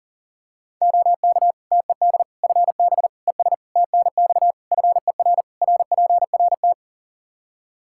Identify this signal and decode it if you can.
Morse code, heard as a keyed tone, about 40 wpm, 700 Hz. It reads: OK TED FB ES TNX FER RPRT
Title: Practice QSO - Exchange 2 - Info - 40wpm